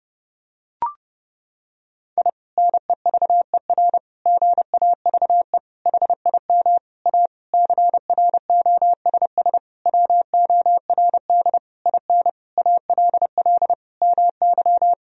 Morse code, heard as a keyed tone, about 30 wpm, 700 Hz. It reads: I NEVER GAVE HIM A CROSS WORD IN ALL MY
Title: Sentences from Top 1000 Words - Review Part 1 - 30wpm